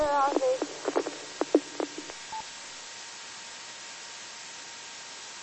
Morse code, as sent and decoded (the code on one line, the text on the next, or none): none